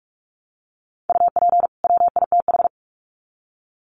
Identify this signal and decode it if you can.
Morse code, heard as a keyed tone, about 44 wpm, 700 Hz. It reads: UP WITH
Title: Sets of 2 Words 45wpm